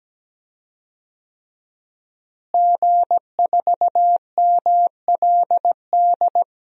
Morse code, heard as a keyed tone, about 17 wpm, 700 Hz. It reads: G4MLD
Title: DXCC-100 Call Signs - 5 Characters - Repeated 3x - 17wpm